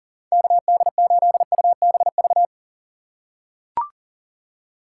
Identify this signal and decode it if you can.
Morse code, heard as a keyed tone, about 40 wpm, 700 Hz. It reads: KD8UBV
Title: USA Call Signs - 6 Characters - 6x Character Spacing - 40wpm